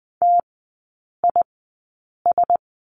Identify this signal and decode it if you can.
Morse code, heard as a keyed tone, about 20 wpm, spaced as for 8 wpm, 700 Hz. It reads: TIS
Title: 041 Lesson 8e - Number 1 - Three Characters 20wpm